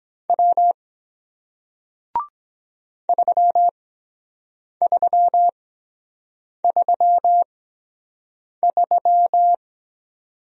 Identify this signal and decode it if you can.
Morse code, heard as a keyed tone, about 24 wpm, spaced as for 7 wpm, 700 Hz. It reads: W 3333